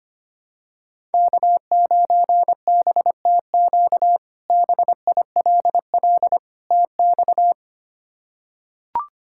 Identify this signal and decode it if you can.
Morse code, heard as a keyed tone, about 25 wpm, 700 Hz. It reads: K9BTQ BILL TX
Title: K1USN SST QSO Practice - 25wpm